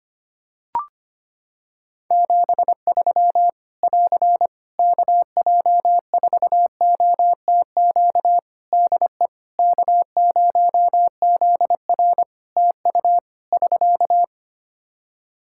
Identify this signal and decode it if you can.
Morse code, heard as a keyed tone, about 25 wpm, 700 Hz. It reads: 73 <AR> KJ4OTQ DE K0ZR TU <SK>
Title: Practice QSO - Exchange 4 - Ending - 25wpm